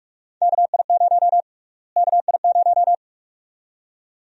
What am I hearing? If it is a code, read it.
Morse code, heard as a keyed tone, about 44 wpm, 700 Hz. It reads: KI0 KI0
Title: USA Call Sign Prefixes - Repeated 2x - 45wpm